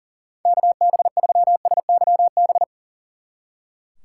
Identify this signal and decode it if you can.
Morse code, heard as a keyed tone, about 38 wpm, 700 Hz. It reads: KB3SYB
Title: Speed Racing - USA Call Signs 40wpm